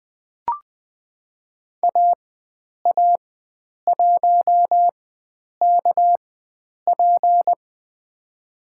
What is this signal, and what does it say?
Morse code, heard as a keyed tone, about 20 wpm, spaced as for 9 wpm, 700 Hz. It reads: AA1KP
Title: USA Call Signs - 5 Characters - 4x Character Spacing - 20wpm